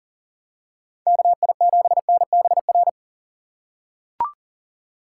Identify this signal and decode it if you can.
Morse code, heard as a keyed tone, about 40 wpm, 700 Hz. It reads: KI7NBR
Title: USA Call Signs - 6 Characters - 2x Character Spacing - 40wpm